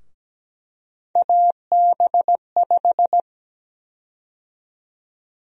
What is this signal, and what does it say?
Morse code, heard as a keyed tone, about 17 wpm, 700 Hz. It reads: AB5
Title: USA Call Sign Prefixes - Repeated 2x - 17wpm